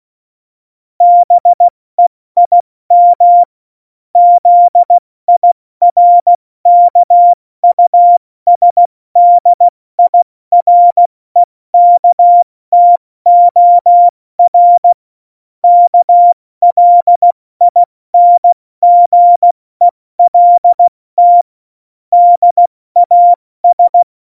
Morse code, heard as a keyed tone, about 16 wpm, spaced as for 14 wpm, 700 Hz. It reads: BEIM ZIRKUSDIREKTOR KLINGELT DAS
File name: Morsekurs / Witze / Witz 02. Oktober 2018, 70 F80